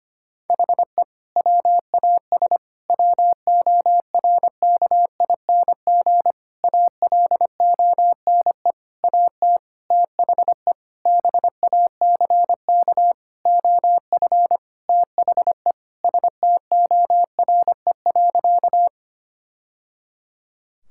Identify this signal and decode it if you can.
Morse code, heard as a keyed tone, about 25 wpm, 700 Hz. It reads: HE WAS WORKING ALONE AT THE BACK OF THE STORE.
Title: Sentences from Top 1000 Words - Review Part 5 - 25wpm